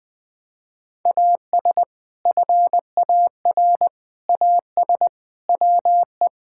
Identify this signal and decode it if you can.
Morse code, heard as a keyed tone, about 21 wpm, 700 Hz. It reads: AS FAR AS WE
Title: Sentences with 6-Word Phrases - 20wpm